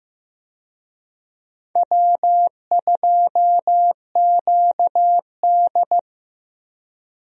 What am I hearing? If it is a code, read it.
Morse code, heard as a keyed tone, about 15 wpm, 700 Hz. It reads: W2QD
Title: USA Call Signs - 4 Characters - Repeated 3x - 15wpm